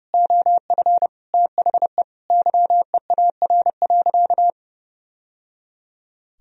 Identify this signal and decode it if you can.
Morse code, heard as a keyed tone, about 30 wpm, 700 Hz. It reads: OF THE YEAR.
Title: Sentences from Top 1000 Words - Review Part 1 - 30wpm